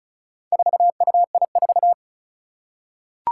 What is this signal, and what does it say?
Morse code, heard as a keyed tone, about 36 wpm, 700 Hz. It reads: 4UI4
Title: Four Challenging Letter-Number Combinations 35wpm